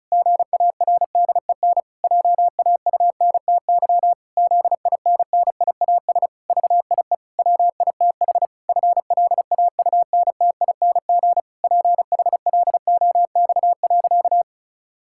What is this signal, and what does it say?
Morse code, heard as a keyed tone, about 36 wpm, 700 Hz. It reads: GARDEN JAUNTY ZINNIAS VIE WITH FLAUNTING PHLOX.
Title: Pangram Challenge v2 - 35wpm